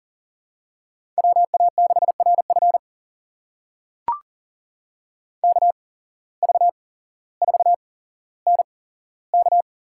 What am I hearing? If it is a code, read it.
Morse code, heard as a keyed tone, about 40 wpm, spaced as for 17 wpm, 700 Hz. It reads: WA6RF KV4NK
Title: USA Call Signs - 5 Characters - 8x Character Spacing - 40wpm